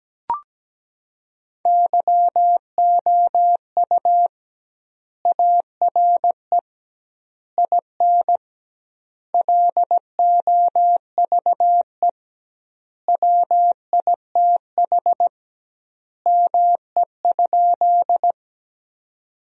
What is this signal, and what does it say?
Morse code, heard as a keyed tone, about 17 wpm, 700 Hz. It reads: YOU ARE IN LOVE WITH ME?
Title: 2x Word-Spacing - Sentences from Top 400 Words - 17wpm